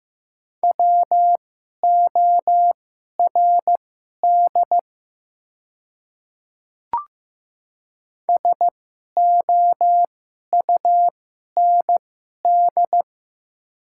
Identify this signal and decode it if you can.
Morse code, heard as a keyed tone, about 15 wpm, spaced as for 11 wpm, 700 Hz. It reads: WORD SOUND
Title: 2x Character Spacing - Top 300 Words - 15wpm